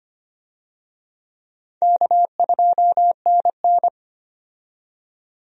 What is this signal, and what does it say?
Morse code, heard as a keyed tone, about 25 wpm, 700 Hz. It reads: K2NN
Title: USA Call Signs - 4 Characters - 25wpm